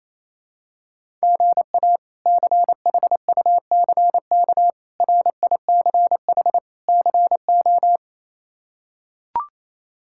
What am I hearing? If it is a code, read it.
Morse code, heard as a keyed tone, about 28 wpm, 700 Hz. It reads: GA CHUCK RICH CO
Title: K1USN SST QSO Practice - 28wpm